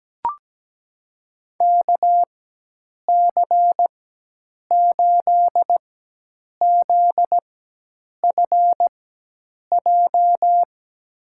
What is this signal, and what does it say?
Morse code, heard as a keyed tone, about 17 wpm, spaced as for 8 wpm, 700 Hz. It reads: KC8ZFJ
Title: USA Call Signs - 4x Character Spacing - 17wpm